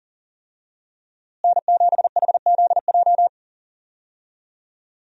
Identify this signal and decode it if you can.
Morse code, heard as a keyed tone, about 40 wpm, 700 Hz. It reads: N7HZJ
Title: USA Call Signs - 5 Characters - Repeated 3x - 40wpm